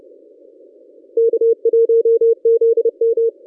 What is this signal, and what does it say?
Morse code, heard as a keyed tone, about 26 wpm, 440 Hz. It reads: K1ZM
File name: contests / ss-cw-k5tr-op-n5rz -1100